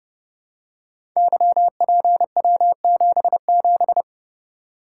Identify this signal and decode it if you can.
Morse code, heard as a keyed tone, about 30 wpm, 700 Hz. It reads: YPW77
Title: Code Groups - 5 Letter:Numbers - 30wpm